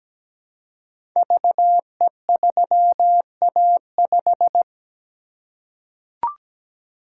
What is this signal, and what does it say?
Morse code, heard as a keyed tone, about 17 wpm, 700 Hz. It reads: VE3A5
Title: Code Groups - 5 Letter:Numbers - 17wpm